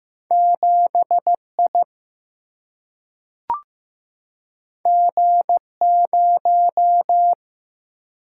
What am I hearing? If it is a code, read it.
Morse code, heard as a keyed tone, about 15 wpm, 700 Hz. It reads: 7 I G 0
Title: Single-Letter Single-Number 15wpm